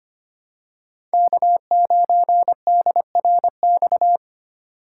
Morse code, heard as a keyed tone, about 25 wpm, 700 Hz. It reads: K9DRX
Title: USA Call Signs - 5 Characters - 3x Character Spacing - 25wpm